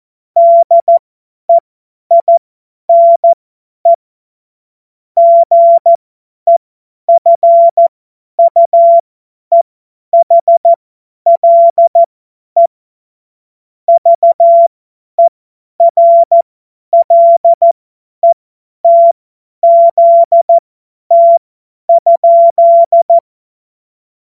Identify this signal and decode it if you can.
Morse code, heard as a keyed tone, about 14 wpm, spaced as for 10 wpm, 680 Hz. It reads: DEINE GEFUEHLE VERLETZT?
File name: Morsekurs / Witze / Witz 06. September 2022 50 F70